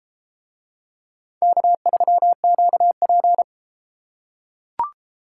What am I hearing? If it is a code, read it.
Morse code, heard as a keyed tone, about 33 wpm, 700 Hz. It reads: K3QP